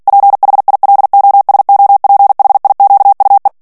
Morse code, heard as a keyed tone, about 48 wpm, 800 Hz. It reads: PHILOSOPHIQUE
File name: CW / FICHIERS-MP3 / FRANCAIS TEXTE CLAIR 2 VITESSE 45 MOTS MINUTES